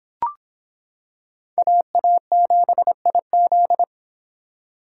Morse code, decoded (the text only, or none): AA7IZ